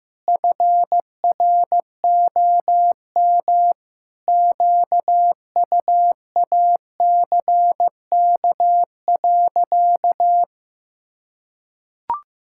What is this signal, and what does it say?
Morse code, heard as a keyed tone, about 15 wpm, 700 Hz. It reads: FROM QUACK.
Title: Pangram Challenge v2 - 15wpm